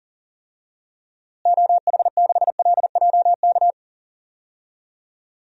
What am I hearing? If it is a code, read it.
Morse code, heard as a keyed tone, about 37 wpm, 700 Hz. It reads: OH6LJK
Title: DXCC-100 Call Signs - 6 Characters - Repeated 2x - 40wpm